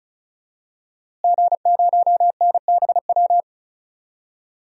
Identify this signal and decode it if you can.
Morse code, heard as a keyed tone, about 36 wpm, 700 Hz. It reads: G0NBW
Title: DXCC-100 Call Signs - 5 Characters - Repeated 3x - 35wpm